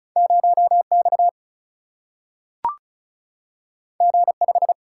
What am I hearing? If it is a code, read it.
Morse code, heard as a keyed tone, about 36 wpm, 700 Hz. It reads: 0 X G 5
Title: Single-Letter Single-Number 35wpm